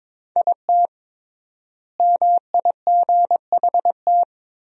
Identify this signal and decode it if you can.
Morse code, heard as a keyed tone, about 22 wpm, 700 Hz. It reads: IT MIGHT
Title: 3x Word-Spacing - Sentences from Top 500 Words - 22wpm